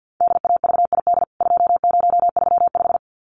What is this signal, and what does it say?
Morse code, heard as a keyed tone, about 50 wpm, 700 Hz. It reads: DAVID 2035
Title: CWT Exchange 50wpm